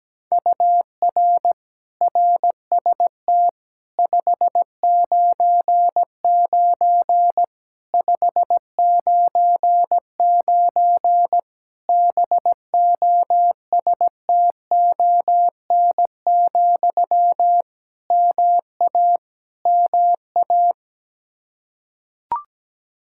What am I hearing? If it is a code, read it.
Morse code, heard as a keyed tone, about 17 wpm, 700 Hz. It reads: UR RST 599 599 BOSTON, MA MA
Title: Practice QSO - Exchange 1 - Beginning - 17wpm